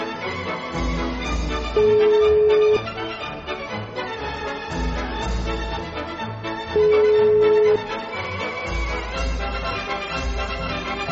Ringback tone (Europe, 425 Hz), with noise.